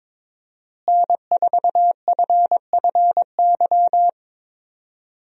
Morse code, heard as a keyed tone, about 22 wpm, 700 Hz. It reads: N4FFY
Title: USA Call Signs - 5 Characters - Repeated 3x - 22wpm